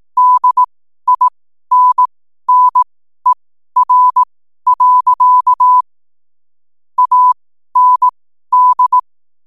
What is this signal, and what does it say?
Morse code, heard as a keyed tone, about 18 wpm, spaced as for 12 wpm, 1000 Hz. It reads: DINNER. AND
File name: n7net / 12wpm